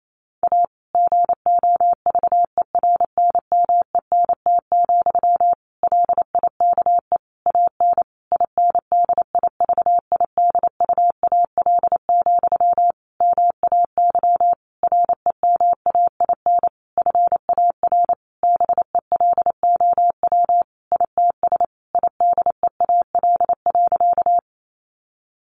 Morse code, heard as a keyed tone, about 28 wpm, 700 Hz. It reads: A GOVERNMENT, LIKE AN INDIVIDUAL, MAY REMAIN FAR BELOW ITS IDEAL.
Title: Sentences from Top 1400 Words 28wpm